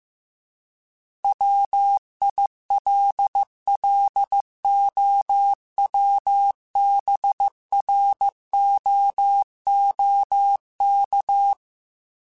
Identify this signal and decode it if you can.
Morse code, heard as a keyed tone, about 15 wpm, 800 Hz. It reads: WILLOWBROOK